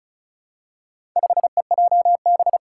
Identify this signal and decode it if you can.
Morse code, heard as a keyed tone, about 34 wpm, 700 Hz. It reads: 5EJB
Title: Four Challenging Letter-Number Combinations 35wpm